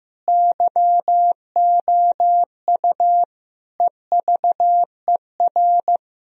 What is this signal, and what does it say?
Morse code, heard as a keyed tone, about 15 wpm, 700 Hz. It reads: YOU EVER
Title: Sentences from Top 1000 Words - Review Part 1 - 15wpm